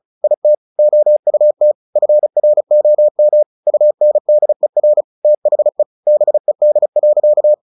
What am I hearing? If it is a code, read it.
Morse code, heard as a keyed tone, about 34 wpm, 600 Hz. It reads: IT OUT FROM UNDER THE BED.
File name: the old man and the sea / the old man and the sea 1 35wpm